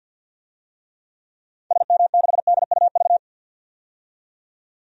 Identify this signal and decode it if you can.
Morse code, heard as a keyed tone, about 53 wpm, 700 Hz. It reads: SM6DUV
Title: DXCC-100 Call Signs - 6 Characters - 50wpm